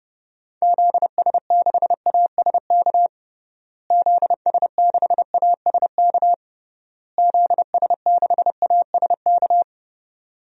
Morse code, heard as a keyed tone, about 32 wpm, 700 Hz. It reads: ZS6ASK ZS6ASK ZS6ASK
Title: DXCC-100 Call Signs - 6 Characters - Repeated 3x - 30wpm